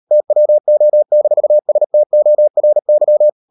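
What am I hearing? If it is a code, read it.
Morse code, heard as a keyed tone, about 38 wpm, 600 Hz. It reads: TWO-STORY
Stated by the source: Morse code practice at 38 WPM, sending Ken Follett's novel Never in English, from CW med Gnister.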